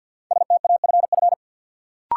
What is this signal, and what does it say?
Morse code, heard as a keyed tone, about 53 wpm, 700 Hz. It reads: STAFF
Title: Sentences from Top 1200 Words - New Word Drill Alt - 50wpm